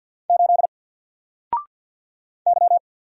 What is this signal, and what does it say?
Morse code, heard as a keyed tone, about 48 wpm, 700 Hz. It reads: 8 Y
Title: Single Letter-Number Alt 50wpm